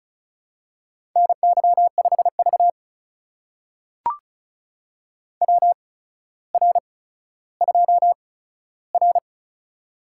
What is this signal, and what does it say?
Morse code, heard as a keyed tone, about 36 wpm, spaced as for 15 wpm, 700 Hz. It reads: NY5V WR2R
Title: USA Call Signs - 4 Characters - 8x Character Spacing - 35wpm